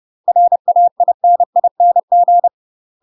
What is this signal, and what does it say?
Morse code, heard as a keyed tone, about 30 wpm, 700 Hz. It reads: RAINING
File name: UK MASTER QSOs TRAINING / 30WPM / UK MASTER QSOs 0048